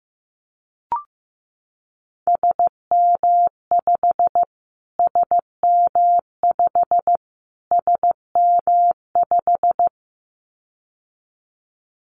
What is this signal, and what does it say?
Morse code, heard as a keyed tone, about 15 wpm, 700 Hz. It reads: SM5 SM5 SM5
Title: Callsign Prefixes for Top 25 DXCC - Repeated 3x - 15wpm